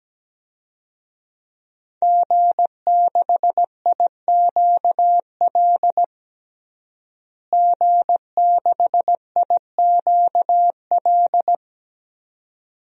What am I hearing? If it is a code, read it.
Morse code, heard as a keyed tone, about 17 wpm, 700 Hz. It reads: G6IQL G6IQL
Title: DXCC-100 Call Signs - 5 Characters - Repeated 2x - 17wpm